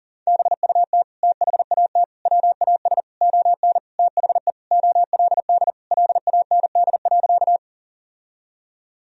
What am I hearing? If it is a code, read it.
Morse code, heard as a keyed tone, about 40 wpm, 700 Hz. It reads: BUT THAT WAS ON THE OLD LAND.
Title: Sentences from Top 1000 Words - Review Part 1 - 40wpm